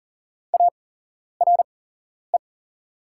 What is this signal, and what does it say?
Morse code, heard as a keyed tone, about 40 wpm, spaced as for 11 wpm, 700 Hz. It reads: ARE
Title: 8x Character Spacing - Top 200 Words - 40wpm